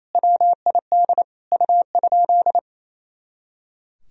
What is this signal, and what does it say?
Morse code, heard as a keyed tone, about 28 wpm, 700 Hz. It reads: WID U?
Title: Sentences from Top 500 Words and Abbreviations 28wpm